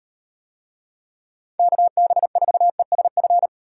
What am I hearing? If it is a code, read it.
Morse code, heard as a keyed tone, about 38 wpm, 700 Hz. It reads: KB4ESF